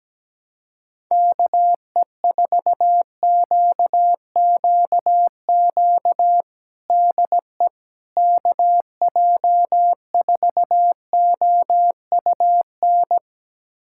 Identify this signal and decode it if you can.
Morse code, heard as a keyed tone, about 17 wpm, 700 Hz. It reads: KE4QQQ DE KJ4OUN
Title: USA Callsign DE Callsign - 6 Characters - 17wpm